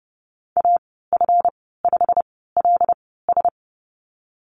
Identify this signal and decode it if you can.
Morse code, heard as a keyed tone, about 30 wpm, spaced as for 17 wpm, 700 Hz. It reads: AF5LS
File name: Lesson 37g - Letter x - Call Signs 30wpm